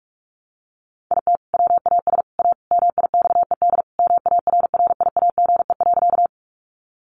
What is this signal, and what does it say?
Morse code, heard as a keyed tone, about 44 wpm, 700 Hz. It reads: IT WAS A MIXED MARRIAGE.
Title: Sentences from Top 1500 Words 45wpm